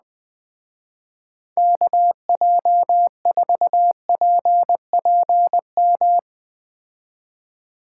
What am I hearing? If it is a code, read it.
Morse code, heard as a keyed tone, about 20 wpm, 700 Hz. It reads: KJ4PPM
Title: Speed Racing - USA Call Signs 17wpm